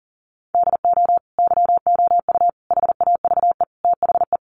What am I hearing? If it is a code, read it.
Morse code, heard as a keyed tone, about 40 wpm, 690 Hz. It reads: DO YOU HAVE THE
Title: Sentences from Top 800 Words 40wpm